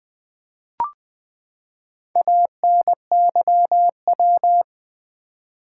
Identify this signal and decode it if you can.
Morse code, heard as a keyed tone, about 20 wpm, 700 Hz. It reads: ANYW
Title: Build A Word - 4 Letters At A Time - 8-Letter Words - 20wpm